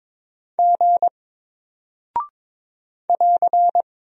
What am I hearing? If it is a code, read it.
Morse code, heard as a keyed tone, about 22 wpm, 700 Hz. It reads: G <AR>
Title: Single Character (Letter:Number:Punctuation:Prosign) Alt 22wpm